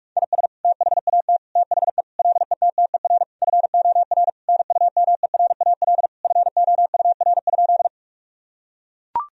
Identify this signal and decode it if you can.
Morse code, heard as a keyed tone, about 42 wpm, 700 Hz. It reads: IS THAT THE LETTER FOR NUMERAL FOUR?
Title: Sentences from Top 1000 Words - Review Part 2 - 45wpm